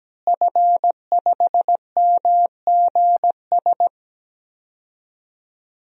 Morse code, heard as a keyed tone, about 17 wpm, 700 Hz. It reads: F5MGS
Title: DXCC-100 Call Signs - 5 Characters - Repeated 3x - 17wpm